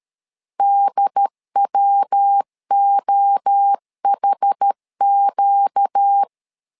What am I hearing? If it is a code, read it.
Morse code, heard as a keyed tone, about 13 wpm, 800 Hz. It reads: DWOHQ
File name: LEZIONI / 4 Q1S3 12wpm 2023